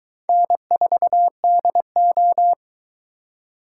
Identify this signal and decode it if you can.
Morse code, heard as a keyed tone, about 22 wpm, 700 Hz. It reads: N4DO